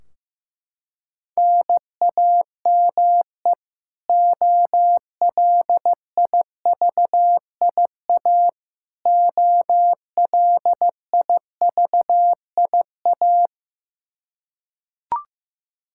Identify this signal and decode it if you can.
Morse code, heard as a keyed tone, about 15 wpm, 700 Hz. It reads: NAME OLIVIA OLIVIA
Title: Practice QSO - Exchange 1 - Beginning - 15wpm